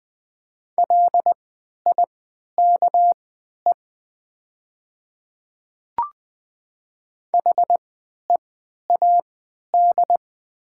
Morse code, heard as a keyed tone, about 20 wpm, spaced as for 11 wpm, 700 Hz. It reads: LIKE HEAD
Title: 3x Character Spacing - Top 300 Words - 20wpm